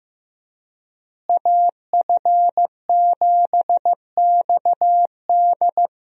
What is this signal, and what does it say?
Morse code, heard as a keyed tone, about 15 wpm, 700 Hz. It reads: AF7XD
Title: USA Call Signs - 5 Characters - Repeated 2x - 15wpm